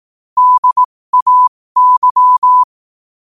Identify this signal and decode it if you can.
Morse code, heard as a keyed tone, about 18 wpm, spaced as for 15 wpm, 1000 Hz. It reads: DAY
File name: qotdfiles / 15wpm